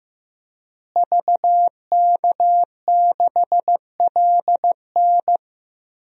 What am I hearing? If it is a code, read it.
Morse code, heard as a keyed tone, about 15 wpm, 700 Hz. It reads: VK6LN
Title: DXCC-100 Call Signs - 5 Characters - Repeated 3x - 15wpm